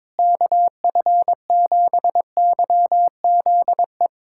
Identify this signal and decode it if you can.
Morse code, heard as a keyed tone, about 22 wpm, 700 Hz. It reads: KF7YZE
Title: USA Callsign DE Callsign 22wpm